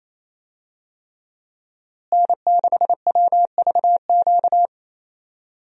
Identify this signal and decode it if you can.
Morse code, heard as a keyed tone, about 28 wpm, 700 Hz. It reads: N6WVQ
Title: USA Call Signs - 5 Characters - Repeated 3x - 28wpm